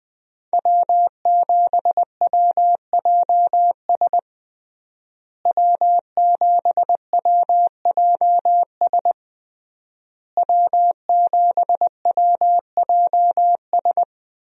Morse code, heard as a keyed tone, about 21 wpm, 700 Hz. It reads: W7WJS W7WJS W7WJS
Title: USA Call Signs - 5 Characters - Repeated 3x - 20wpm